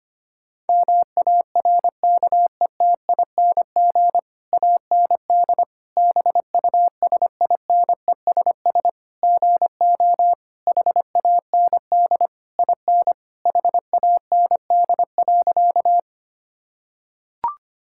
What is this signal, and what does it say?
Morse code, heard as a keyed tone, about 25 wpm, 700 Hz. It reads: MARKETING AND BUSINESS GO HAND IN HAND.
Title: Sentences from Top 1000 Words - Review Part 5 - 25wpm